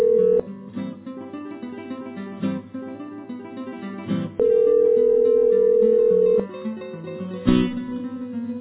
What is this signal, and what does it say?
Ringback tone (North America), 440 + 480 Hz.